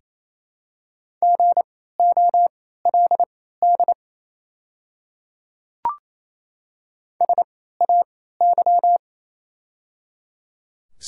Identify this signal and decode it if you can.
Morse code, heard as a keyed tone, about 28 wpm, spaced as for 16 wpm, 700 Hz. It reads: GOLD SAY
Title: 3x Character Spacing - Top 500 Words - 28wpm